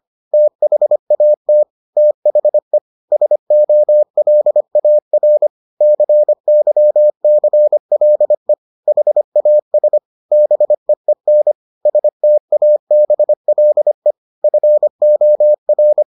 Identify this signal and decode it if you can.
Morse code, heard as a keyed tone, about 25 wpm, 600 Hz. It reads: THAT THE SOLAR CYCLE HAS BEEN STABLE FOR